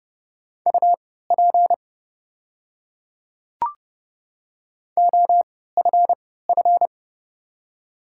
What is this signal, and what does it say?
Morse code, heard as a keyed tone, about 30 wpm, spaced as for 17 wpm, 690 Hz. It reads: UP OFF
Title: 3x Character Spacing - Top 200 Words - 30wpm